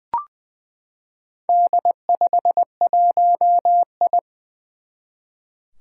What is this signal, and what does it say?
Morse code, heard as a keyed tone, about 21 wpm, 700 Hz. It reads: D51I
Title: Four Challenging Letter-Number Combinations 20wpm